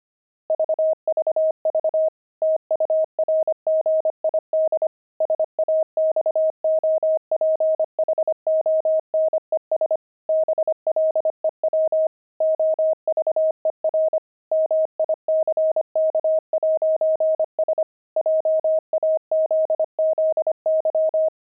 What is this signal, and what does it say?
Morse code, heard as a keyed tone, about 25 wpm, 630 Hz. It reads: VVV TURGID SAXOPHONES BLEW OVER MICK'S JAZZY
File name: pangram-fs-25-cs-25-saxophones